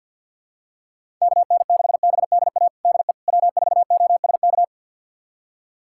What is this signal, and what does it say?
Morse code, heard as a keyed tone, about 51 wpm, 700 Hz. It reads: KN6BDA DE W4OSK
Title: USA Callsign DE Callsign 50wpm